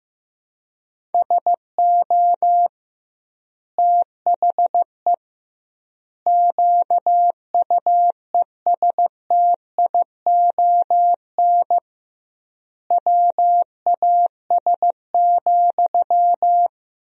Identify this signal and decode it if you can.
Morse code, heard as a keyed tone, about 16 wpm, 700 Hz. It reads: SO THE QUESTION WAS,
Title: 2x Word-Spacing - Sentences from Top 400 Words - 15wpm